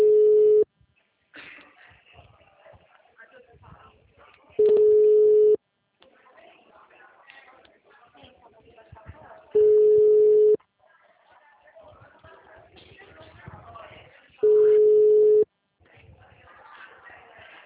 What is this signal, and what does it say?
Ringback tone (Europe, 425 Hz), with speech.
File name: out- -2009